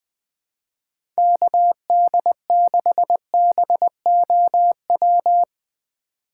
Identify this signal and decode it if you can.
Morse code, heard as a keyed tone, about 20 wpm, 700 Hz. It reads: KD6BOW
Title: USA Call Signs - 4x Character Spacing - 20wpm